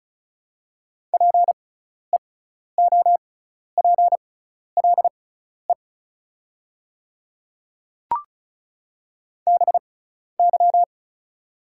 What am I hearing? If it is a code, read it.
Morse code, heard as a keyed tone, about 34 wpm, spaced as for 12 wpm, 700 Hz. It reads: PEOPLE BY
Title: 6x Character Spacing - Top 100 Words - 35wpm